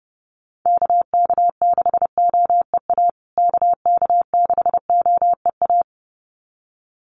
Morse code, heard as a keyed tone, about 30 wpm, 700 Hz. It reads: KK6OEA KK6OEA
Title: USA Call Signs Repeated 2x 30wpm